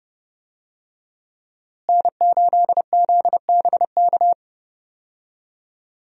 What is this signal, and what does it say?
Morse code, heard as a keyed tone, about 29 wpm, 700 Hz. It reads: N8ZBK
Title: USA Call Signs - 5 Characters - Repeated 3x - 30wpm